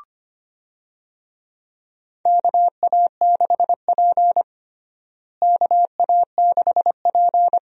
Morse code, heard as a keyed tone, about 25 wpm, 700 Hz. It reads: KA6P KA6P
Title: USA Call Signs - 4 Characters - Repeated 2x - 25wpm